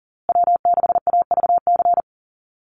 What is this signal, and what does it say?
Morse code, heard as a keyed tone, about 40 wpm, 700 Hz. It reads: W6AVC
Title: USA Call Signs Repeated 3x 40wpm